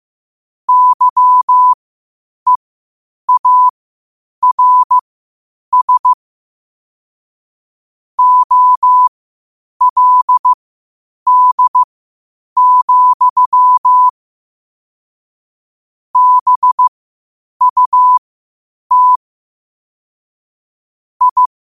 Morse code, heard as a keyed tone, about 15 wpm, spaced as for 8 wpm, 1000 Hz. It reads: YEARS OLD, BUT I
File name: qotdfiles / 8wpm